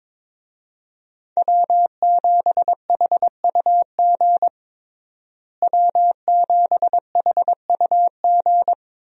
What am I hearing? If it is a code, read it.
Morse code, heard as a keyed tone, about 22 wpm, 700 Hz. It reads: W7HUG W7HUG
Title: USA Call Signs - 5 Characters - Repeated 2x - 22wpm